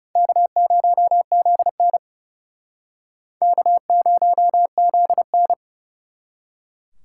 Morse code, heard as a keyed tone, about 40 wpm, 700 Hz. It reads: K0ZN K0ZN